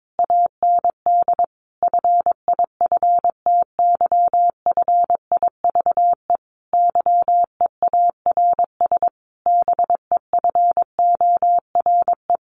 Morse code, heard as a keyed tone, about 22 wpm, 700 Hz. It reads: AND FIFTY FIVE YEARS BEFORE
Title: A Sound Of Thunder 22wpm